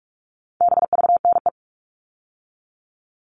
Morse code, heard as a keyed tone, about 44 wpm, 700 Hz. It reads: BVNE
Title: All Cut Numbers - 4-Digit 45wpm